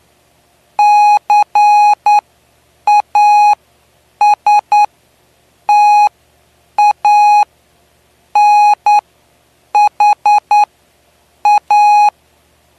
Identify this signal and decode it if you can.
Morse code, heard as a keyed tone, about 9 wpm, spaced as for 7 wpm, 830 Hz. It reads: CASTANHA
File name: cw / py2lcd / 09-PY2LCD-Li��o 9